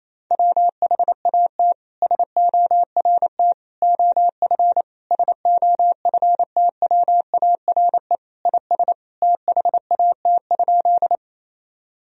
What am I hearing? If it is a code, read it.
Morse code, heard as a keyed tone, about 28 wpm, 700 Hz. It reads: WHAT SORT OF SOFTWARE IS THAT?
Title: Sentences from Top 1000 Words - Review Part 3 - 28wpm